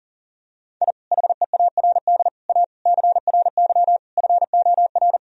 Morse code, heard as a keyed tone, about 40 wpm, 700 Hz. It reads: I HEARD A CRY FOR
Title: Sentences from Top 1000 Words - Review Part 2 - 40wpm